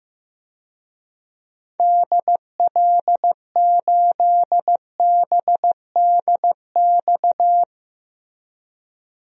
Morse code, heard as a keyed tone, about 15 wpm, 700 Hz. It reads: DL8BDX
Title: DXCC-100 Call Signs - 6 Characters - Repeated 3x - 15wpm